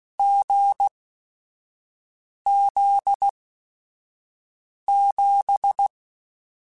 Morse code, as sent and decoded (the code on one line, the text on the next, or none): --. --.. --...
GZ7